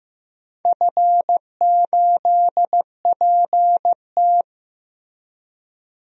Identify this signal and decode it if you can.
Morse code, heard as a keyed tone, about 15 wpm, 700 Hz. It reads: F8PT
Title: DXCC-100 Call Signs - 4 Characters - Repeated 3x - 15wpm